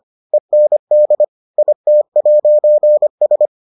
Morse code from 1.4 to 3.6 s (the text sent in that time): IT'S